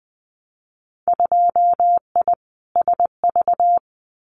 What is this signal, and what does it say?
Morse code, heard as a keyed tone, about 19 wpm, 700 Hz. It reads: 2I SV
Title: Winter Field Day Exchanges Encore 20wpm